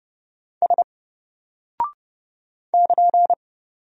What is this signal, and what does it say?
Morse code, heard as a keyed tone, about 29 wpm, 700 Hz. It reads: S <KN>
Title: Single Character (Letter:Number:Punctuation:Prosign) Alt 30wpm